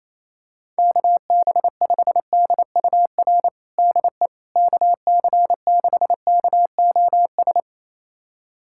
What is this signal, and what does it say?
Morse code, heard as a keyed tone, about 28 wpm, 700 Hz. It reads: KB5DUR DE KC6KOS
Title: USA Callsign DE Callsign - 6 Characters - 28wpm